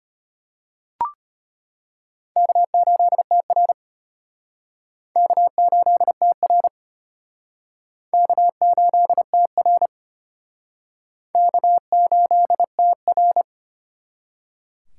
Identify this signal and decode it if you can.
Morse code, heard as a keyed tone, about 31 wpm, 700 Hz. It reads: K8TR K8TR K8TR K8TR
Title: Speed Racing - USA 1x2 Call Signs 25wpm